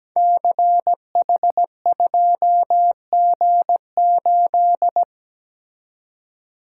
Morse code, heard as a keyed tone, about 17 wpm, 700 Hz. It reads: CH2G8
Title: Code Groups - 5 Letter:Numbers - 17wpm